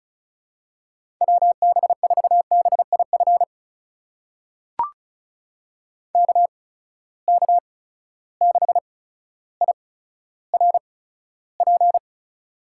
Morse code, heard as a keyed tone, about 34 wpm, spaced as for 15 wpm, 700 Hz. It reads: WB4BIF KK6IRP
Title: USA Call Signs - 6 Characters - 8x Character Spacing - 35wpm